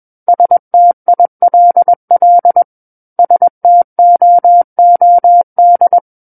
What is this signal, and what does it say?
Morse code, heard as a keyed tone, about 21 wpm, 700 Hz. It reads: STILL STOOD
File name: Book One / 12-Chapter 12 - 21 WPM